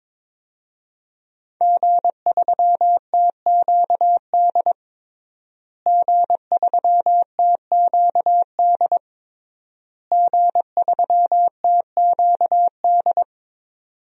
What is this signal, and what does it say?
Morse code, heard as a keyed tone, about 22 wpm, 700 Hz. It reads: G3TQD G3TQD G3TQD
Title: DXCC-100 Call Signs - 5 Characters - Repeated 3x - 22wpm